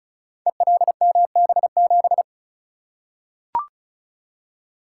Morse code, heard as a keyed tone, about 35 wpm, 700 Hz. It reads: ELMB7
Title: Code Groups - 5 Letter:Numbers - 35wpm